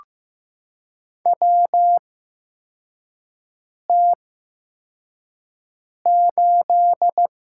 Morse code, heard as a keyed tone, about 15 wpm, spaced as for 4 wpm, 700 Hz. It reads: WT8
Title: USA Call Signs - 4 Characters - 8x Character Spacing - 15wpm